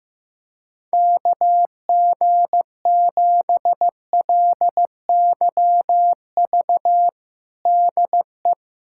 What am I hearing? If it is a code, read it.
Morse code, heard as a keyed tone, about 15 wpm, 700 Hz. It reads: KG7LYV DE
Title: USA Callsign DE Callsign - 6 Characters - 15wpm